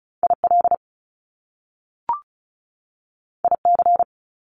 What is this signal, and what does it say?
Morse code, heard as a keyed tone, about 36 wpm, 700 Hz. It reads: IL IC
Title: Top Two Letter Combinations in Words 35wpm